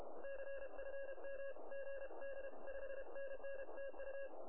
Morse code, heard as a keyed tone, about 32 wpm, 1650 Hz, with radio noise. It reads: CFM GD 5NN TU
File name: attachments / QSK